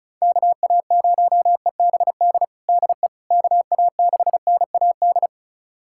Morse code, heard as a keyed tone, about 36 wpm, 700 Hz. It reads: KA0EBD DE KA6NAD
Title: USA Callsign DE Callsign - 6 Characters - 35wpm